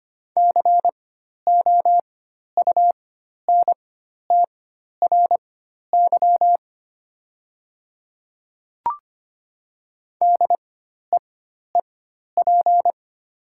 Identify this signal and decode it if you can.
Morse code, heard as a keyed tone, about 25 wpm, spaced as for 12 wpm, 700 Hz. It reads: COUNTRY DEEP
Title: 4x Character Spacing - Top 500 Words - 25wpm